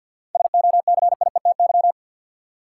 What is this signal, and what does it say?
Morse code, heard as a keyed tone, about 50 wpm, 700 Hz. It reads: SOCIETY
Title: Sentences from Top 700 Words - New Word Drill - 50wpm